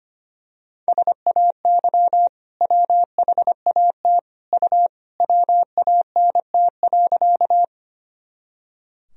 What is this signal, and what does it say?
Morse code, heard as a keyed tone, about 25 wpm, 700 Hz. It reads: SAY WHAT U WANT.
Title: Sentences from Top 200 Words and Abbreviations 25wpm